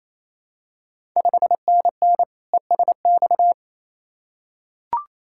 Morse code, heard as a keyed tone, about 28 wpm, 700 Hz. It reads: 5NN ESX
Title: Ontario QSO Party Exchange 28wpm